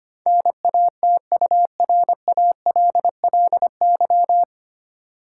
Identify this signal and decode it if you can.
Morse code, heard as a keyed tone, about 25 wpm, 700 Hz. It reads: NATURALLY
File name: practice / 25wpm / 9 Letter Words - First 3 Letters Given - 25wpm